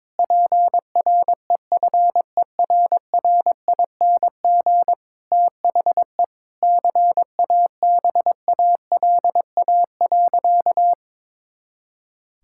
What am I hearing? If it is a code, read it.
Morse code, heard as a keyed tone, about 22 wpm, 700 Hz. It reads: PREFERRING THE CABALA.
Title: Pangram Challenge v2 - 22wpm